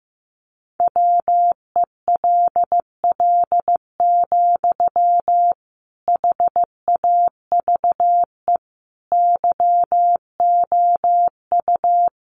Morse code, heard as a keyed tone, about 15 wpm, 700 Hz. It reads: WELL, HAVE YOU
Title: Sentences from Top 800 Words 15wpm